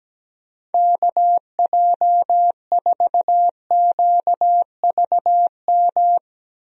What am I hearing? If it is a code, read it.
Morse code, heard as a keyed tone, about 17 wpm, 700 Hz. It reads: KJ4QVM
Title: USA Call Signs - 6 Characters - Repeated 2x - 17wpm